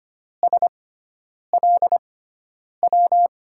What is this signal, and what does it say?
Morse code, heard as a keyed tone, about 25 wpm, spaced as for 9 wpm, 700 Hz. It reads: SLW
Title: USA Call Signs - 6 Characters - 6x Character Spacing - 25wpm